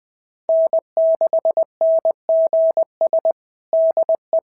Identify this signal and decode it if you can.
Morse code, heard as a keyed tone, about 20 wpm, 650 Hz. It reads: N6NGS DE
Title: SOTA QSO Practice 20wpm